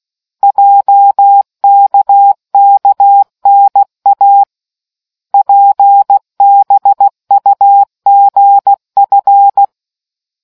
Morse code, heard as a keyed tone, about 16 wpm, 800 Hz. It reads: JKKNA PBUGF